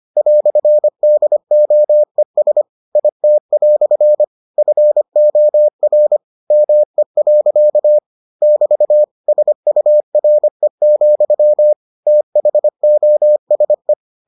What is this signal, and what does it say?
Morse code, heard as a keyed tone, about 25 wpm, 600 Hz. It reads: "DOES IT" FOR ME. <BT> SURE, THOSE